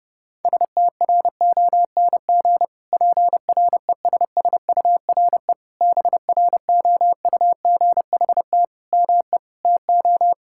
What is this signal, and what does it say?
Morse code, heard as a keyed tone, about 30 wpm, 700 Hz. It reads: STRONG PRESSURE BROUGHT ME TO
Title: Sentences from Top 1000 Words - Review Part 6 - 30wpm